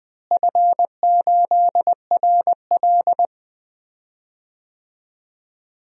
Morse code, heard as a keyed tone, about 20 wpm, 700 Hz. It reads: F8RL
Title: DXCC-100 Call Signs - 4 Characters - Repeated 2x - 20wpm